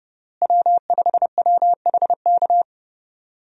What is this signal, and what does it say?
Morse code, heard as a keyed tone, about 30 wpm, 700 Hz. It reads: W5WHK
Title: USA Call Signs - 5 Characters - Repeated 3x - 30wpm